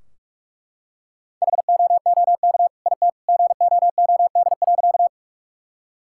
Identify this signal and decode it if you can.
Morse code, heard as a keyed tone, about 42 wpm, 700 Hz. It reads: HOOK IT GOOD.
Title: Sentences from Top 1600 Words 45wpm